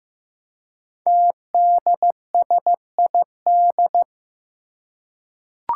Morse code, heard as a keyed tone, about 15 wpm, 700 Hz. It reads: TDSID
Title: Five Challenging Letter-Number Combinations 15wpm